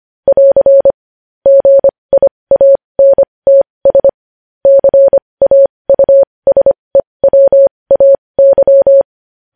Morse code, heard as a keyed tone, about 25 wpm, spaced as for 20 wpm, 550 Hz. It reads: <AR> GIANTS CAUSEWAY
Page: news-headlines-in-morse-code-at-20-wpm